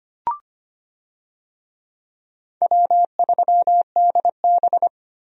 Morse code, heard as a keyed tone, about 25 wpm, 700 Hz. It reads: W3DB
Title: USA Call Signs - 4 Characters - 25wpm